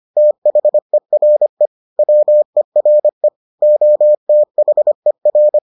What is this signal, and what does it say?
Morse code, heard as a keyed tone, about 25 wpm, 600 Hz. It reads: THERE WERE OTHER